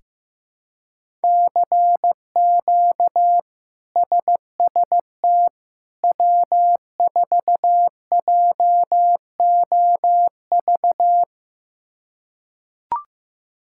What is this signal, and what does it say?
Morse code, heard as a keyed tone, about 16 wpm, 700 Hz. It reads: CQ SST W4JOV
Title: K1USN SST QSO Practice - 15wpm